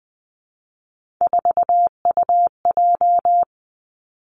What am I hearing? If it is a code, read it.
Morse code, heard as a keyed tone, about 20 wpm, 700 Hz. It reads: 4UJ
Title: Three Challenging Letter-Number Combinations 20wpm